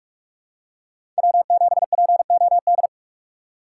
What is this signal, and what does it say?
Morse code, heard as a keyed tone, about 43 wpm, 700 Hz. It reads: W7POD
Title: USA Call Signs - 3x Character Spacing - 45wpm